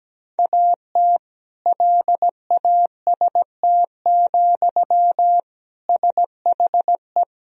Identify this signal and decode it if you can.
Morse code, heard as a keyed tone, about 17 wpm, 700 Hz. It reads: AT LAST, SHE
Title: Sentences from Top 1000 Words - Review Part 2 - 17wpm